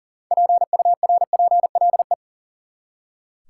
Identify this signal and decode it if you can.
Morse code, heard as a keyed tone, about 40 wpm, 700 Hz. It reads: PURPLE
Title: Six Letter Words - 40wpm